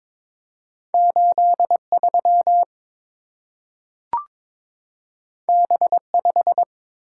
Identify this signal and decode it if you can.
Morse code, heard as a keyed tone, about 22 wpm, 700 Hz. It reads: 83 B5
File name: practice / 22wpm / Two Letter-Number Combinations - 22wpm